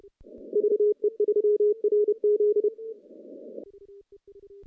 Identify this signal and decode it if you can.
Morse code, heard as a keyed tone, about 29 wpm, 420 Hz. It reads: VE3RZ
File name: contests / CW-SS-k5tr- -0100